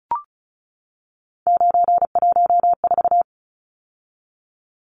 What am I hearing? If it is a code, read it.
Morse code, heard as a keyed tone, about 34 wpm, 700 Hz. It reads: 914
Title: Numbers - 3-Digit 35wpm